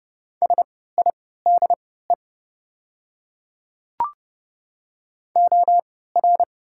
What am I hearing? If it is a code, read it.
Morse code, heard as a keyed tone, about 30 wpm, spaced as for 17 wpm, 690 Hz. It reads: SIDE OR
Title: 3x Character Spacing - Top 100 Words - 30wpm